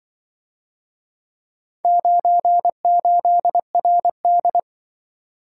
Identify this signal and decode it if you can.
Morse code, heard as a keyed tone, about 24 wpm, 700 Hz. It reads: 98RD
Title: Speed-Racing - Four Letter-Number Combinations - 20wpm